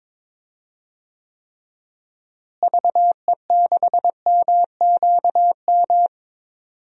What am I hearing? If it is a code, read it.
Morse code, heard as a keyed tone, about 22 wpm, 700 Hz. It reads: VE6MQM
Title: DXCC-100 Call Signs - 6 Characters - Repeated 2x - 22wpm